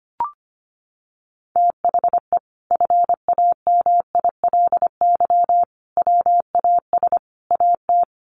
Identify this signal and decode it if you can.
Morse code, heard as a keyed tone, about 25 wpm, 700 Hz. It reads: THE FAMILY WAS AT
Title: Sentences from Top 400 Words 25wpm